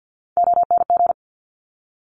Morse code, heard as a keyed tone, about 50 wpm, 700 Hz. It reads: ONG
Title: Top Three Letter Combinations in Words 50wpm